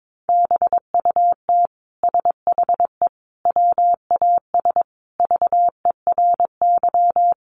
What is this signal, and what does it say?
Morse code, heard as a keyed tone, about 22 wpm, 700 Hz. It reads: BUT SHE WAS VERY
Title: Sentences from Top 1000 Words 22wpm